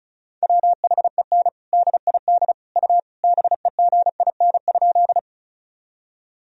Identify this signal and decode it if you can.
Morse code, heard as a keyed tone, about 34 wpm, 700 Hz. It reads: WHEN DID U BEGIN?
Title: Sentences from Top 300 Words and Abbreviations 35wpm